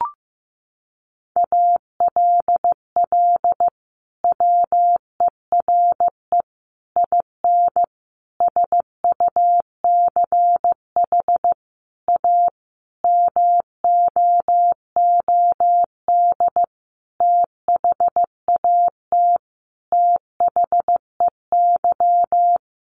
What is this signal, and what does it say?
Morse code, heard as a keyed tone, about 15 wpm, 700 Hz. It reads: ALL WERE IN SUCH A MOOD THAT THEY
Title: Sentences from Top 1100 Words 15wpm